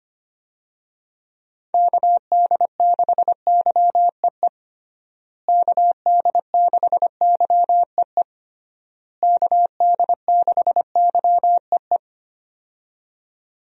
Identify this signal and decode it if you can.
Morse code, heard as a keyed tone, about 25 wpm, 700 Hz. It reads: KD6YEE KD6YEE KD6YEE
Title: USA Call Signs - 6 Characters - Repeated 3x - 25wpm